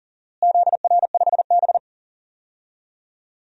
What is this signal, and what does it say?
Morse code, heard as a keyed tone, about 37 wpm, 700 Hz. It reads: ZR5B
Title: DXCC-100 Call Signs - 4 Characters - Repeated 3x - 40wpm